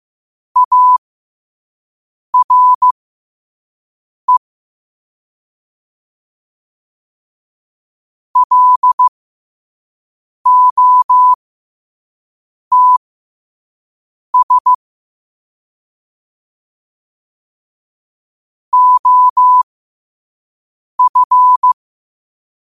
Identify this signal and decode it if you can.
Morse code, heard as a keyed tone, about 15 wpm, spaced as for 5 wpm, 1000 Hz. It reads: ARE LOTS OF
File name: qotdfiles / 5wpm